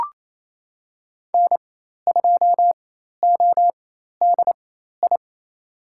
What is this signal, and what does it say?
Morse code, heard as a keyed tone, about 27 wpm, spaced as for 13 wpm, 700 Hz. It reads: N2ODI
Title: USA Call Signs - 5 Characters - 4x Character Spacing - 28wpm